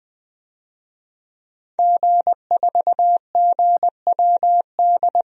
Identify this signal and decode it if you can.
Morse code, heard as a keyed tone, about 20 wpm, 700 Hz. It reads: G4GWD
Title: DXCC-100 Call Signs - 5 Characters - Repeated 2x - 20wpm